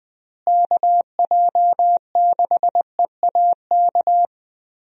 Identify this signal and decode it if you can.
Morse code, heard as a keyed tone, about 20 wpm, 700 Hz. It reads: KJ6EAK
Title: USA Call Signs - 6 Characters - Repeated 2x - 20wpm